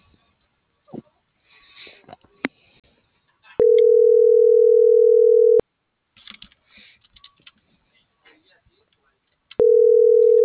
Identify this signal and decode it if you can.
Ringback tone (North America), 440 + 480 Hz.